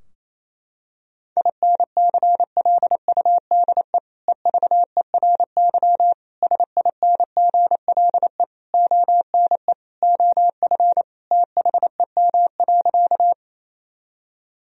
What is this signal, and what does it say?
Morse code, heard as a keyed tone, about 28 wpm, 700 Hz. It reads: INCLUDE EVERY SINGLE ONE OF THEM.
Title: Sentences from Top 1000 Words - Review Part 4 - 28wpm